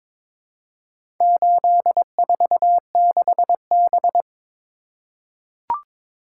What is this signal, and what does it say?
Morse code, heard as a keyed tone, about 22 wpm, 700 Hz. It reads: 846B
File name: practice / 22wpm / Four Letter-Number Combinations - 22wpm